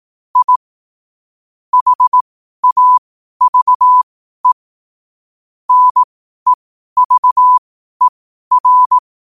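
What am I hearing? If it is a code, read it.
Morse code, heard as a keyed tone, about 18 wpm, spaced as for 12 wpm, 1000 Hz. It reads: I HAVE NEVER
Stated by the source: Morse question-of-the-day sent at 12 wpm.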